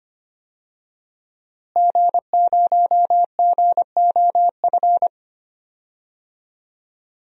Morse code, heard as a keyed tone, about 25 wpm, 700 Hz. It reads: G0GOF
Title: DXCC-100 Call Signs - 5 Characters - Repeated 2x - 25wpm